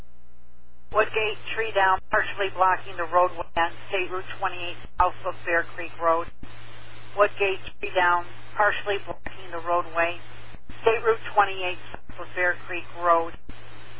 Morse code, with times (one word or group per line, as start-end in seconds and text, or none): none